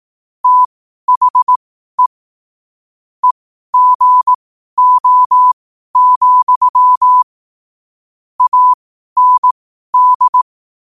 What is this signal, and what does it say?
Morse code, heard as a keyed tone, about 18 wpm, spaced as for 12 wpm, 1000 Hz. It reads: THE EGO, AND
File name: qotdfiles / 12wpm